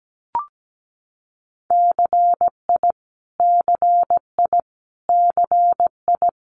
Morse code, heard as a keyed tone, about 17 wpm, 700 Hz. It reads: CI CI CI
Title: Multi-letter Phonemes Repeated 3x 17wpm